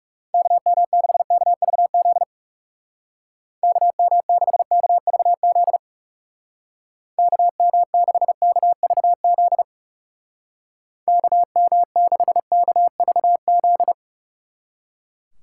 Morse code, heard as a keyed tone, about 36 wpm, 700 Hz. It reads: KM6KVZ KM6KVZ KM6KVZ KM6KVZ
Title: Speed Racing - USA 2x3 Call Signs 30wpm